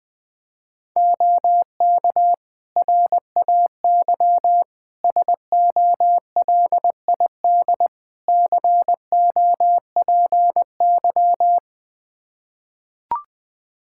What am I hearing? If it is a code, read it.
Morse code, heard as a keyed tone, about 20 wpm, 700 Hz. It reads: OK RAY SOLID COPY
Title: Practice QSO - Exchange 3 - More Info - 20wpm